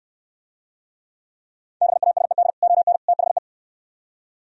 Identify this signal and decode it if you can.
Morse code, heard as a keyed tone, about 68 wpm, 700 Hz. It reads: BASED ON THE